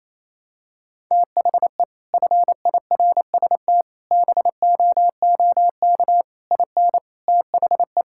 Morse code, heard as a keyed tone, about 28 wpm, 700 Hz. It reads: THE FIRST BOOK IN THE
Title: Sentences from Top 1000 Words - Review Part 5 - 28wpm